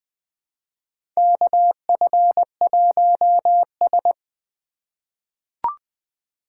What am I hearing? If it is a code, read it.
Morse code, heard as a keyed tone, about 21 wpm, 700 Hz. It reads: KF1S
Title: USA Call Signs - 2x Character Spacing - 20wpm